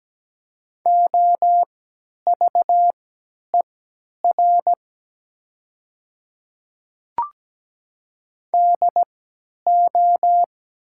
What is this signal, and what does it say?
Morse code, heard as a keyed tone, about 17 wpm, spaced as for 10 wpm, 700 Hz. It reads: OVER DO
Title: 3x Character Spacing - Top 100 Words - 17wpm